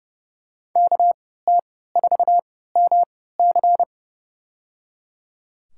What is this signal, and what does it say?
Morse code, heard as a keyed tone, about 31 wpm, spaced as for 17 wpm, 700 Hz. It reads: KT4MC
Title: USA Call Signs - 5 Characters - 3x Character Spacing - 30wpm